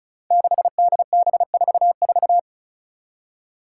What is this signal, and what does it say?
Morse code, heard as a keyed tone, about 36 wpm, 700 Hz. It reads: 6DB44
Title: Five Challenging Letter-Number Combinations 35wpm